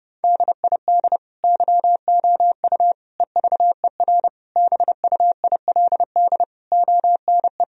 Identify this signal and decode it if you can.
Morse code, heard as a keyed tone, about 30 wpm, 700 Hz. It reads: DID YOU EVER BUILD ONE
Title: Sentences from Top 1000 Words - Review Part 1 - 30wpm